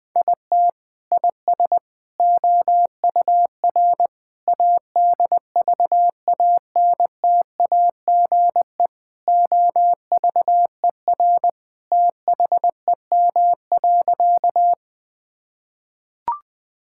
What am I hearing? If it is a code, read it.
Morse code, heard as a keyed tone, about 20 wpm, 700 Hz. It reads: IT IS OUR ADVANTAGE OVER THEM.
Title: Sentences from Top 1000 Words - Review Part 6 - 20wpm